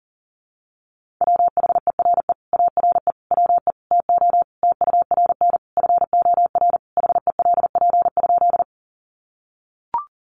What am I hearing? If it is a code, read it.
Morse code, heard as a keyed tone, about 40 wpm, 700 Hz. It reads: WHERE ARE WE TO TURN FOR HELP?
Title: Sentences from Top 200 Words 40wpm